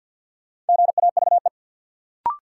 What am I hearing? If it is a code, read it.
Morse code, heard as a keyed tone, about 50 wpm, 700 Hz. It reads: GAVE
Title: Sentences from Top 500 Words - New Word Drill - 50wpm